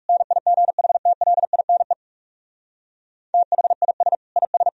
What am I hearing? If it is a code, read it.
Morse code, heard as a keyed tone, about 46 wpm, 700 Hz. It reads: NIGHTLINE THIS IS